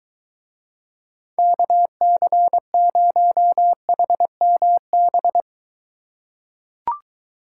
Morse code, heard as a keyed tone, about 23 wpm, 700 Hz. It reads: KC0HMB